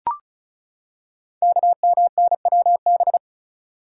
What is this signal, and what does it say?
Morse code, heard as a keyed tone, about 36 wpm, 700 Hz. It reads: KMNWB
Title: Code Groups - 5 Letters - 35wpm